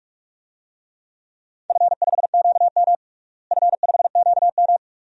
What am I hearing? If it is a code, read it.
Morse code, heard as a keyed tone, about 42 wpm, 700 Hz. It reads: F5QM F5QM
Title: DXCC-100 Call Signs - 4 Characters - Repeated 2x - 45wpm